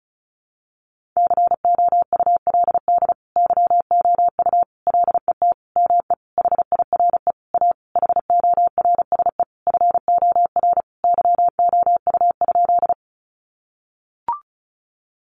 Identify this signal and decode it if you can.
Morse code, heard as a keyed tone, about 34 wpm, 700 Hz. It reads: COULD YOU LET ME HIRE A HORSE FOR YOU?
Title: Sentences from Top 1400 Words 35wpm